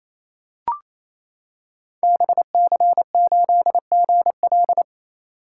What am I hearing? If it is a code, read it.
Morse code, heard as a keyed tone, about 27 wpm, 700 Hz. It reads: BC8GL
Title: Code Groups - 5 Letter:Numbers - 28wpm